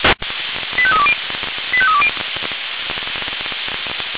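Five-tone selective call (ZVEI), heard as quick runs of five tones, tones of about 70 ms. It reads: 85320 85320
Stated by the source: ZVEI five-tone selective call received on SDR.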